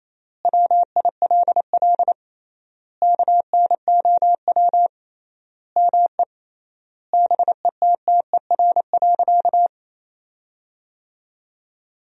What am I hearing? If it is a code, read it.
Morse code, heard as a keyed tone, about 28 wpm, 700 Hz. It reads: WILL KNOW ME BETTER.
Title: 3x Word-Spacing - Sentences from Top 400 Words - 28wpm